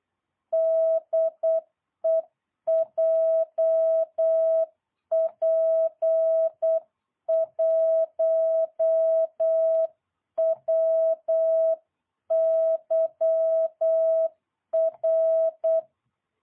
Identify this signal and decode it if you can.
Morse code, heard as a keyed tone, about 8 wpm, 660 Hz. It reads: DE JP1WYR